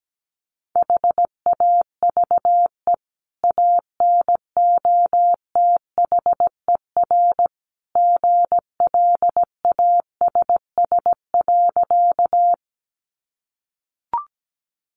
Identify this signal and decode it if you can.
Morse code, heard as a keyed tone, about 17 wpm, 700 Hz. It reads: HAVE ANOTHER GLASS.
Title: Sentences from Top 1300 Words 17wpm